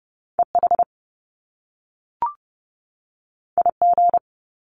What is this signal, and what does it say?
Morse code, heard as a keyed tone, about 30 wpm, 700 Hz. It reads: EH IG
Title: Top Two Letter Combinations in Words 30wpm